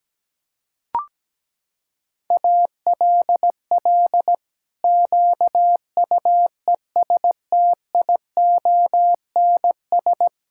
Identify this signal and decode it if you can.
Morse code, heard as a keyed tone, about 17 wpm, 700 Hz. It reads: ALL QUESTIONS
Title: Pangram Challenge v2 - 17wpm